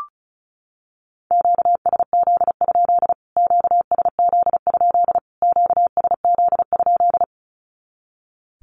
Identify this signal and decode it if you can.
Morse code, heard as a keyed tone, about 34 wpm, 700 Hz. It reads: QSZ? QSZ? QSZ?
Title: Q-Codes Repeated 3x 35wpm